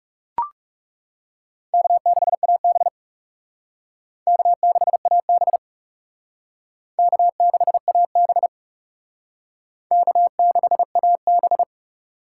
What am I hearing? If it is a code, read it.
Morse code, heard as a keyed tone, about 37 wpm, 700 Hz. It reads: K6AB K6AB K6AB K6AB
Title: Speed Racing - USA Call Signs 30wpm